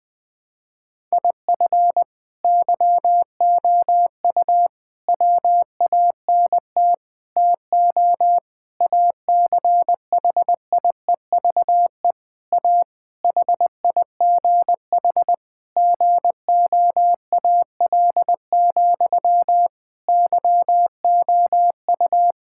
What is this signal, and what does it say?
Morse code, heard as a keyed tone, about 20 wpm, 700 Hz. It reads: IF YOU WANT TO ACHIEVE A HIGH GOAL, YOU
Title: Sentences from Top 1000 Words - Review Part 6 - 20wpm